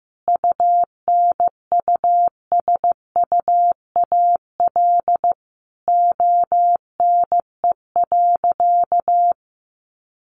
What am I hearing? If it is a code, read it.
Morse code, heard as a keyed tone, about 15 wpm, 700 Hz. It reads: UNUSUAL ONE.
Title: Sentences from Top 1400 Words 15wpm